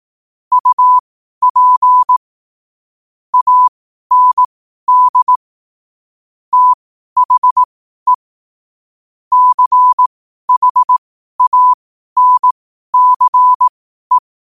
Morse code, heard as a keyed tone, about 18 wpm, spaced as for 12 wpm, 1000 Hz. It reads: UP AND THE CHANCE
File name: qotdfiles / 12wpm